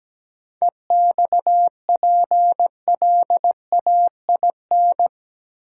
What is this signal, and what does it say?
Morse code, heard as a keyed tone, about 17 wpm, 700 Hz. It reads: EXPLAIN
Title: Seven Letter Words - 17wpm